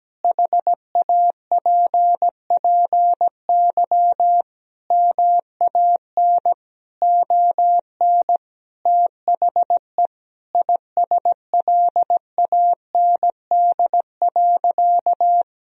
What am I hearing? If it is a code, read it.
Morse code, heard as a keyed tone, about 17 wpm, 700 Hz. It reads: HAPPY MAN ON THE ISLAND.
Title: Sentences from Top 1000 Words - Review Part 3 - 17wpm